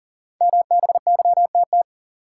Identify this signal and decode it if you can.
Morse code, heard as a keyed tone, about 39 wpm, 700 Hz. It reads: MBYTT
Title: Code Groups - 5 Letter:Numbers - 40wpm